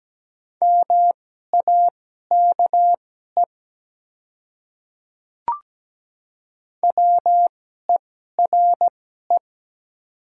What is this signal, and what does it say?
Morse code, heard as a keyed tone, about 17 wpm, spaced as for 12 wpm, 690 Hz. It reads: MAKE WERE
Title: 2x Character Spacing - Top 100 Words - 17wpm